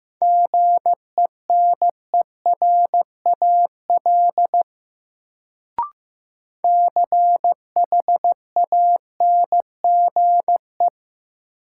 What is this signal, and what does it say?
Morse code, heard as a keyed tone, about 15 wpm, 700 Hz. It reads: GENERAL CHANGE
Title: Sentences from Top 600 Words - New Word Drill - 15wpm